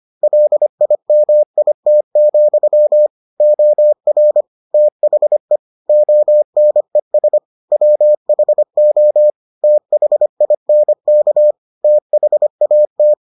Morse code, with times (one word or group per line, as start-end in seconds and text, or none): 0.2-3.1 LIMIT,
3.4-4.4 OR
4.7-5.6 THE
5.9-7.4 ONES
7.7-9.3 WHO
9.6-11.5 THINK
11.8-13.1 THAT